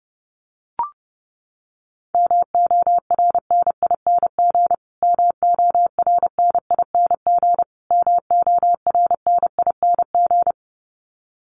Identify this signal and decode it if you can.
Morse code, heard as a keyed tone, about 30 wpm, 700 Hz. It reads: MORNING MORNING MORNING
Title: Top 500 Words Repeated 30wpm